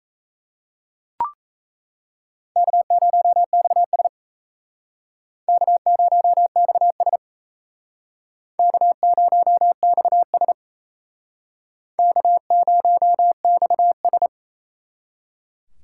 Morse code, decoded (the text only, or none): K0XS K0XS K0XS K0XS